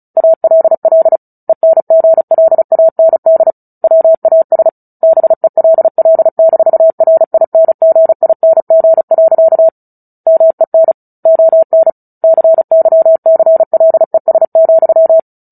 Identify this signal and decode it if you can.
Morse code, heard as a keyed tone, about 34 wpm, 650 Hz. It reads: ALL ENGLAND WAS BELL-RINGING. MEN ON CYCLES,
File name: Book Two / 09-Chapter 9 - 35 WPM